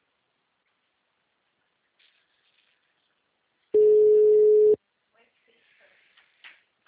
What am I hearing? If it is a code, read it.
Ringback tone (Europe, 425 Hz), with speech.
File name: out- -2005- - - .55753